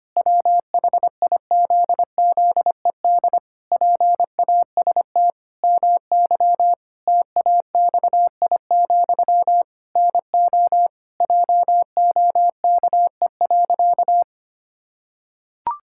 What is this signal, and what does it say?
Morse code, heard as a keyed tone, about 25 wpm, 700 Hz. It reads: WHIZZED PAST MY TAXI, NO JOKE.
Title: Pangram Challenge v2 - 25wpm